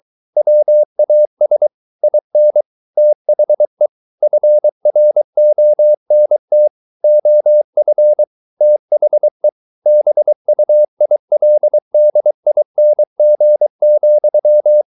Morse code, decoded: WAS IN THE FRONT OF THE BUILDING,

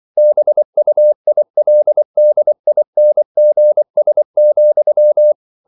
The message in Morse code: BUILDINGS,

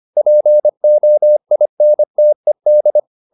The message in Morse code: POINTED

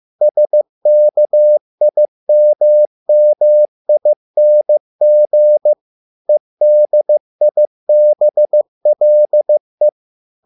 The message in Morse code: SKIMMING EDIBLE

